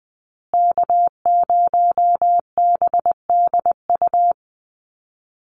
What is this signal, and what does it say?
Morse code, heard as a keyed tone, about 20 wpm, 700 Hz. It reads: K0BDU
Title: USA Call Signs Repeated 3x 20wpm